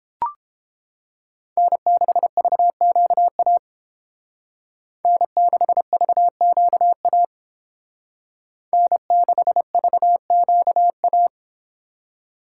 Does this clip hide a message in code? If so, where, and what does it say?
Yes: Morse code from 1.6 to 11.3 s, heard as a keyed tone, about 28 wpm, 700 Hz. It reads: N6VQA N6VQA N6VQA